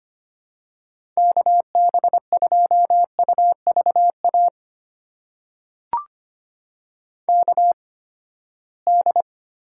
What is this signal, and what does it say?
Morse code, heard as a keyed tone, about 25 wpm, spaced as for 17 wpm, 700 Hz. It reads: KB2UVA KD
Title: USA Call Signs - 8x Character Spacing - 25wpm